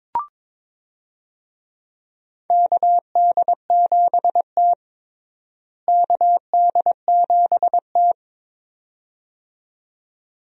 Morse code, heard as a keyed tone, about 21 wpm, 700 Hz. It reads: KD7T KD7T
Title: USA Call Signs - 4 Characters - Repeated 2x - 22wpm